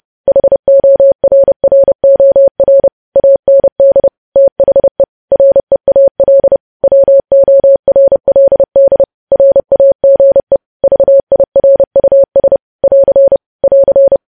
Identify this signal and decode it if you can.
Morse code, heard as a keyed tone, about 30 wpm, 550 Hz. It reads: HORROR AND THE REAL WORLD RAGE VIRUS <AR> <AR>
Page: news headlines in morse code at 30 wpm